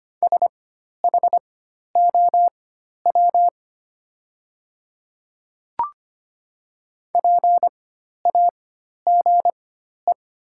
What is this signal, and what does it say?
Morse code, heard as a keyed tone, about 25 wpm, spaced as for 12 wpm, 700 Hz. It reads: SHOW PAGE
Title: 4x Character Spacing - Top 500 Words - 25wpm